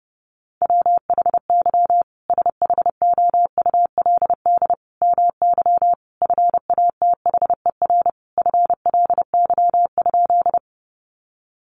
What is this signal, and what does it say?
Morse code, heard as a keyed tone, about 28 wpm, 700 Hz. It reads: WHY SHOULD MY FATHER FLY?
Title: Sentences from Top 500 Words 30wpm